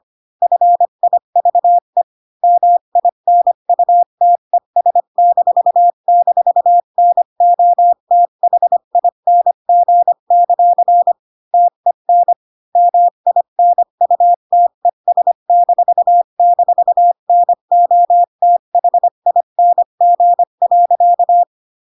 Morse code, heard as a keyed tone, about 25 wpm, 700 Hz. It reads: FIVE MINUTES--NOTHING; TEN MINUTES--NOTHING.
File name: Pinocchio / Pinocchio ch1 20 25 ik1hge